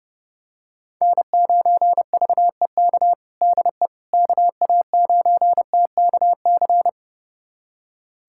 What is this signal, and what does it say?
Morse code, heard as a keyed tone, about 30 wpm, 700 Hz. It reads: N9VEK DE KA9TKC
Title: USA Callsign DE Callsign 30wpm